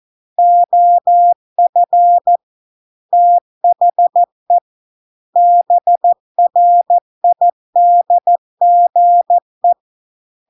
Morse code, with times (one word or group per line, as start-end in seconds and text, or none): 0.4-2.4 OF
3.1-4.6 THE
5.4-9.7 BRIDGE